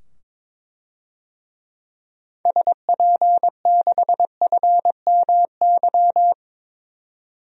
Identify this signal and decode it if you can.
Morse code, heard as a keyed tone, about 22 wpm, 700 Hz. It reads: SP6FMY
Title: DXCC-100 Call Signs - 6 Characters - 22wpm